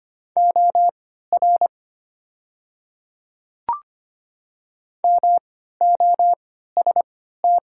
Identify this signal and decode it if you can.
Morse code, heard as a keyed tone, about 25 wpm, spaced as for 14 wpm, 700 Hz. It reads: OR MOST
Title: 3x Character Spacing - Top 100 Words - 25wpm